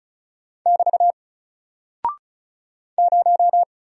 Morse code, heard as a keyed tone, about 36 wpm, 700 Hz. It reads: <BT> 0
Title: Single Character (Letter:Number:Punctuation:Prosign) Alt 35wpm